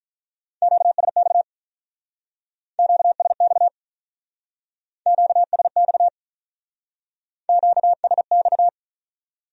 Morse code, heard as a keyed tone, about 43 wpm, 700 Hz. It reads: QSX QSX QSX QSX
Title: Speed-Racing Q-Codes - 35wpm